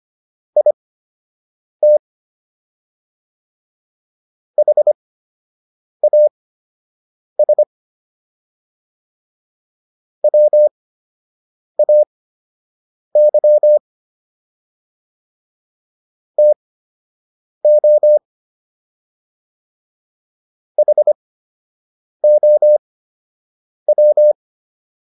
Morse code, heard as a keyed tone, about 25 wpm, spaced as for 7 wpm, 600 Hz. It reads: IT HAS WAY TO HOW